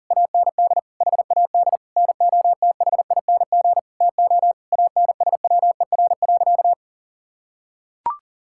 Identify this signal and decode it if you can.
Morse code, heard as a keyed tone, about 40 wpm, 700 Hz. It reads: AND HAD NOTHING TO ANSWER.
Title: Sentences from Top 1600 Words 40wpm